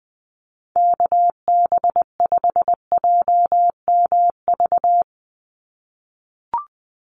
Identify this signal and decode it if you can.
Morse code, heard as a keyed tone, about 20 wpm, 700 Hz. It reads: KB5JMV
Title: USA Call Signs 20wpm